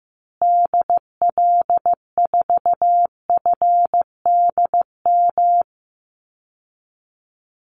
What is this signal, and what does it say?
Morse code, heard as a keyed tone, about 14 wpm, 700 Hz. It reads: DL4FDM
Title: Call Signs Repeated 2x - 15wpm